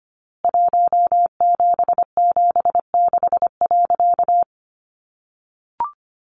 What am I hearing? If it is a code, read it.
Morse code, heard as a keyed tone, about 25 wpm, 700 Hz. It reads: 1776.
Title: A Sound Of Thunder 25wpm